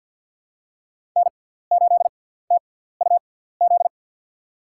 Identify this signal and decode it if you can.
Morse code, heard as a keyed tone, about 50 wpm, spaced as for 17 wpm, 700 Hz. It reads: N8TUZ
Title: USA Call Signs - 6x Character Spacing - 50wpm